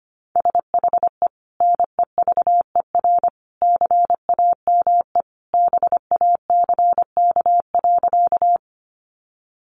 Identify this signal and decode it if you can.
Morse code, heard as a keyed tone, about 25 wpm, 700 Hz. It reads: SHE NEVER CAME BACK.
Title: Sentences from Top 300 Words 25wpm